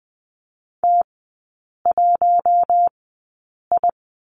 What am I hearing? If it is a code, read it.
Morse code, heard as a keyed tone, about 20 wpm, spaced as for 8 wpm, 700 Hz. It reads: T1I
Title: 074 Lesson 13e - Letter L - Three Characters 20wpm